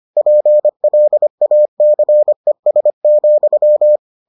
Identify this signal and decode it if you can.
Morse code, heard as a keyed tone, about 25 wpm, 600 Hz. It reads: PLACES,